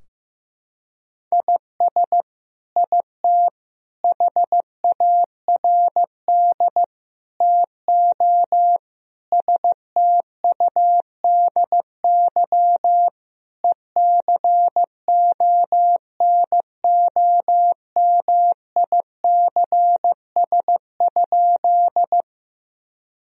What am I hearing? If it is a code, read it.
Morse code, heard as a keyed tone, about 16 wpm, 700 Hz. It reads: IS IT HARD TO STUDY ECONOMICS?
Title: Sentences from Top 1000 Words - Review Part 3 - 15wpm